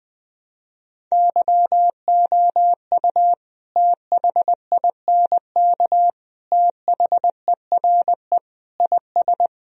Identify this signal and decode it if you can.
Morse code, heard as a keyed tone, about 20 wpm, 700 Hz. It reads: YOU THINK THERE IS
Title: Sentences from Top 1000 Words - Review Part 4 - 20wpm